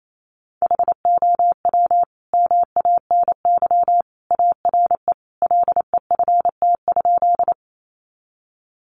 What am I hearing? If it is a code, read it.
Morse code, heard as a keyed tone, about 28 wpm, 700 Hz. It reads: HOW MANY ARE LEFT?
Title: Sentences from Top 300 Words 28wpm